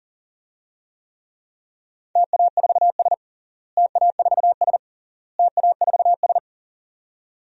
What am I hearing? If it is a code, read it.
Morse code, heard as a keyed tone, about 40 wpm, 700 Hz. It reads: TA4S TA4S TA4S
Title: DXCC-100 Call Signs - 4 Characters - Repeated 3x - 40wpm